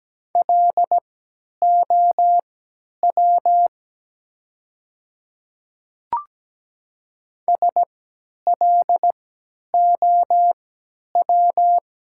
Morse code, heard as a keyed tone, about 17 wpm, spaced as for 10 wpm, 700 Hz. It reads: LOW SLOW
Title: 3x Character Spacing - Top 500 Words - 17wpm